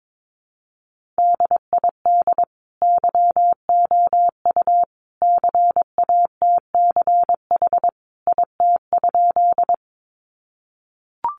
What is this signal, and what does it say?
Morse code, heard as a keyed tone, about 22 wpm, 700 Hz. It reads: DID YOU CATCH IT?
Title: Sentences from Top 1500 Words 22wpm